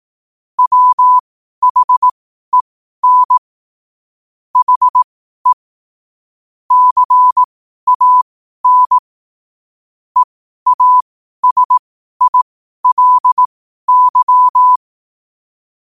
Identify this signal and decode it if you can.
Morse code, heard as a keyed tone, about 18 wpm, spaced as for 12 wpm, 1000 Hz. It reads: WHEN HE CAN EASILY
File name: qotdfiles / 12wpm